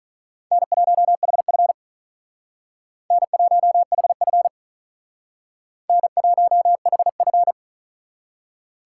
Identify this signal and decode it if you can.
Morse code, heard as a keyed tone, about 42 wpm, 700 Hz. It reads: N1HF N1HF N1HF
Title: Speed Racing - USA 1x2 Call Signs 35wpm